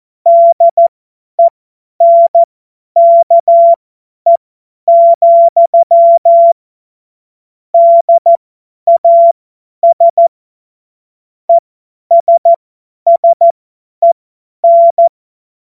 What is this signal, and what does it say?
Morse code, heard as a keyed tone, about 14 wpm, spaced as for 10 wpm, 680 Hz. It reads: DENKE, DAS ESSEN
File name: Morsekurs / Witze / Witz 21. März 2023 50 F70